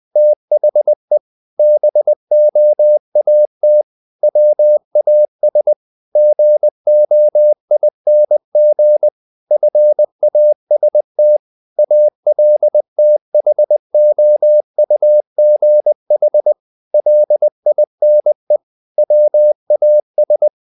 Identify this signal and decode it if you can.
Morse code, heard as a keyed tone, about 20 wpm, 600 Hz. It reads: THE BOAT WAS GOING FAST ALTHOUGH LINE WAS
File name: the old man and the sea part 2 20wpm